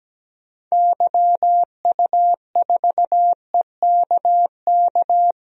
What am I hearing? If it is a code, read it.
Morse code, heard as a keyed tone, about 17 wpm, 700 Hz. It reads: YU4EKK
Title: DXCC-100 Call Signs - 6 Characters - 17wpm